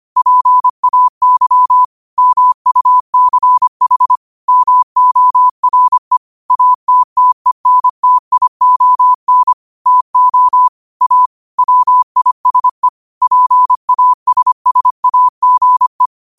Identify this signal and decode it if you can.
Morse code, heard as a keyed tone, about 25 wpm, 1000 Hz. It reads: PAY MUCH MORE ATTENTION TO A WISE PASSAGE
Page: Quotes for Wed, 13 Aug 2025 in Morse Code at 25 words per minute.